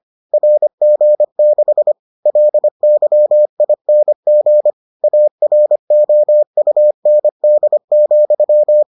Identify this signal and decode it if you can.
Morse code, heard as a keyed tone, about 25 wpm, 600 Hz. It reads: RG6 LYING AROUND,